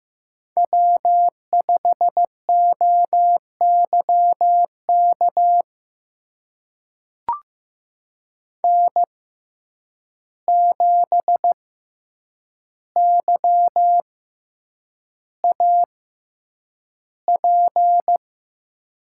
Morse code, heard as a keyed tone, about 15 wpm, spaced as for 8 wpm, 700 Hz. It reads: W5OYK N7YAP
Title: USA Call Signs - 5 Characters - 6x Character Spacing - 15wpm